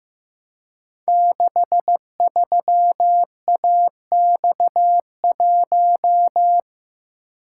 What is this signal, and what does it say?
Morse code, heard as a keyed tone, about 15 wpm, 700 Hz. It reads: 63AX1
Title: Code Groups - 5 Letter:Numbers - 15wpm